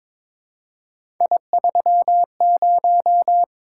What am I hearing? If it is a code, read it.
Morse code, heard as a keyed tone, about 22 wpm, 700 Hz. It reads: I30
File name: practice / 22wpm / Three Letter-Number Combinations - 22wpm